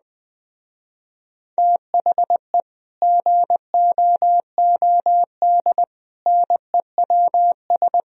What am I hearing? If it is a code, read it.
Morse code, heard as a keyed tone, about 20 wpm, 700 Hz. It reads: THE GOOD NEWS
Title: Speed-Racing - Sets of 5 Words 17wpm